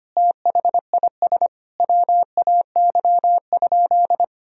Morse code, heard as a keyed tone, about 25 wpm, 700 Hz. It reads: THIS WAY?
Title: Sentences from Top 1000 Words - Review Part 1 - 25wpm